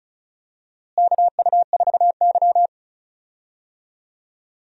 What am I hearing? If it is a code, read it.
Morse code, heard as a keyed tone, about 36 wpm, 700 Hz. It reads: KU4Y
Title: USA Call Signs - 4 Characters - Repeated 2x - 35wpm